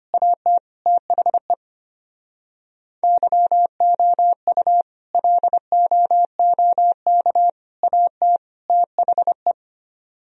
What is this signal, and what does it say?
Morse code, heard as a keyed tone, about 32 wpm, 700 Hz. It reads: AT THE YOU LOOK AT THE